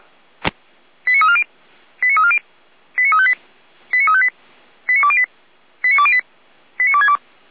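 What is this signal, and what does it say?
Five-tone selective call (ZVEI), heard as quick runs of five tones, tones of about 70 ms. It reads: 89340 89340 89358 89358 89208 89208 89262
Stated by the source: ZVEI five-tone selective call 89140 received on SDR, Feuerwehr Orlamünde fire brigade alerting.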